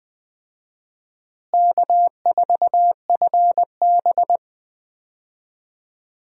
Morse code, heard as a keyed tone, about 20 wpm, 700 Hz. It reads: K4FB
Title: USA Call Signs - 4 Characters - Repeated 3x - 20wpm